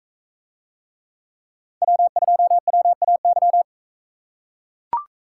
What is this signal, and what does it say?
Morse code, heard as a keyed tone, about 44 wpm, 700 Hz. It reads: W2WAY